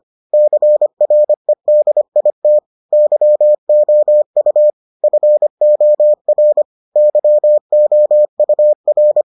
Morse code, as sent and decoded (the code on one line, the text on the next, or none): -.-. .-. -.. .. - / -.-- --- ..- / ..-. --- .-. / -.-- --- ..- .-.
CREDIT YOU FOR YOUR